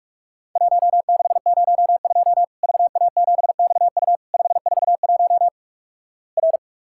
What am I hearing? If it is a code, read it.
Morse code, heard as a keyed tone, about 48 wpm, 700 Hz. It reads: 1602 VA7XU 541 R
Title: Worked All Europe DX Contest - QTC Exchange - 45wpm